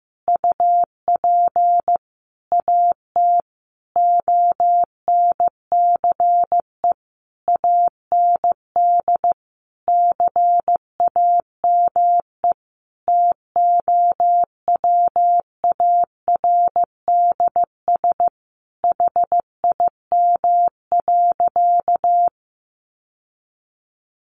Morse code, heard as a keyed tone, about 15 wpm, 700 Hz. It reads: UP AT ONCE AND CAME TOWARDS HIM.
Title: Sentences from Top 1100 Words 15wpm